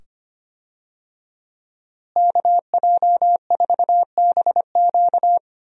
Morse code, heard as a keyed tone, about 25 wpm, 700 Hz. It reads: KJ4BQ
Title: USA Call Signs - 5 Characters - Repeated 2x - 25wpm